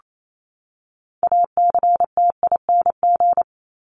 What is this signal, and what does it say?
Morse code, heard as a keyed tone, about 28 wpm, 700 Hz. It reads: ACTING
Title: ING Suffix - Top 100 Words with Base Words 28wpm